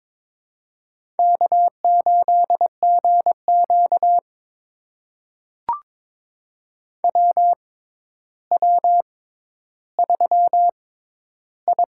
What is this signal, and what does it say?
Morse code, heard as a keyed tone, about 22 wpm, spaced as for 11 wpm, 700 Hz. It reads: K8GQ WW3I
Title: USA Call Signs - 4 Characters - 6x Character Spacing - 22wpm